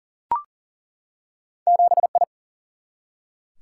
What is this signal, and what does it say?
Morse code, heard as a keyed tone, about 40 wpm, 700 Hz. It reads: 7 I
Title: Single-Letter Single-Number 40wpm